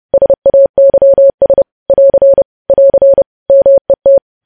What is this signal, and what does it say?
Morse code, heard as a keyed tone, about 30 wpm, 550 Hz. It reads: SAYS <AR> <AR> MET
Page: news headlines in morse code at 30 wpm